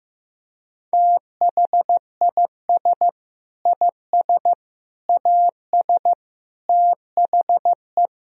Morse code, heard as a keyed tone, about 15 wpm, 700 Hz. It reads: THIS IS AS THE
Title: Sentences from Top 1000 Words - Review Part 3 - 15wpm